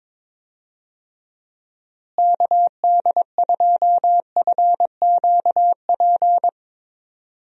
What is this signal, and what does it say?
Morse code, heard as a keyed tone, about 22 wpm, 700 Hz. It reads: KD2FQP
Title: USA Call Signs - 6 Characters - Repeated 3x - 22wpm